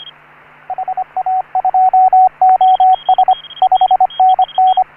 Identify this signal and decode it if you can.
Morse code, heard as a keyed tone, about 27 wpm, 750 Hz, with radio noise. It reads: HA2OS 5NN
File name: CW / HA3OU RHOBCW REC